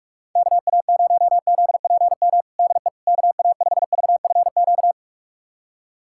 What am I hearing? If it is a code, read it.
Morse code, heard as a keyed tone, about 46 wpm, 700 Hz. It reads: KA0ZPM DE KA5VFQ
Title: USA Callsign DE Callsign - 6 Characters - 45wpm